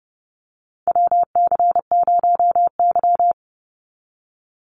Morse code, heard as a keyed tone, about 30 wpm, 700 Hz. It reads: WC0Y
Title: NA SOTA Chaser Callsigns 30wpm